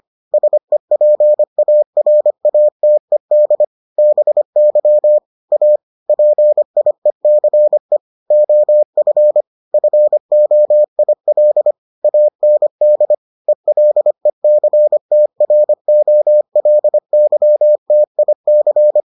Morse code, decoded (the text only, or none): SEPARATED BY A PIECE OF FOIL AND ELECTROLYTIC